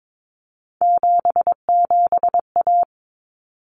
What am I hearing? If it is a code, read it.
Morse code, heard as a keyed tone, about 22 wpm, 700 Hz. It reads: 77A
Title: Common Cut Numbers - 3-Digit 22wpm